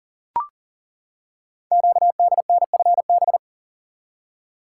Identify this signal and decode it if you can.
Morse code, heard as a keyed tone, about 38 wpm, 700 Hz. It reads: QDNFB
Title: Code Groups - 5 Letters - 40wpm